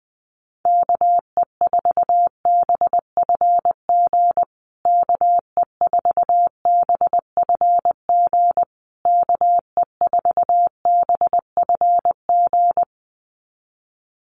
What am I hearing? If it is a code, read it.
Morse code, heard as a keyed tone, about 20 wpm, 700 Hz. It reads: KE4BFG KE4BFG KE4BFG
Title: USA Call Signs Repeated 3x 20wpm